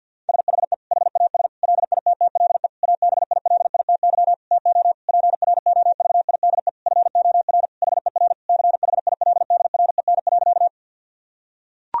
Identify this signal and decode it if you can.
Morse code, heard as a keyed tone, about 51 wpm, 700 Hz. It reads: SHE HAS LITTLE ABILITY TO PROVIDE FOR HER CHILDREN.
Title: Sentences from Top 1000 Words - Review Part 3 - 50wpm